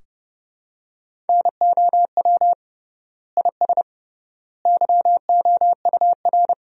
Morse code, heard as a keyed tone, about 30 wpm, 700 Hz. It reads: NOW IS YOUR
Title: 3x Word-Spacing - Sentences from Top 100 Words - 30wpm